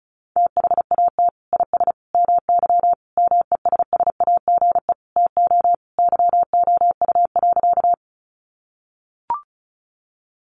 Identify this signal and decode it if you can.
Morse code, heard as a keyed tone, about 34 wpm, 700 Hz. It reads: THAT IS MY MESSAGE TO YOU.
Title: Sentences from Top 1100 Words 35wpm